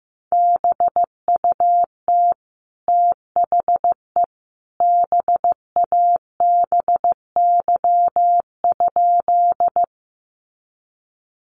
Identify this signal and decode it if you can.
Morse code, heard as a keyed tone, about 15 wpm, 700 Hz. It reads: BUT THE BABY?
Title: Sentences from Top 1300 Words 15wpm